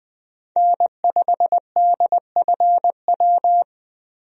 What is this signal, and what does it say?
Morse code, heard as a keyed tone, about 20 wpm, 700 Hz. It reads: N5DFW
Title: USA Call Signs - 5 Characters - 20wpm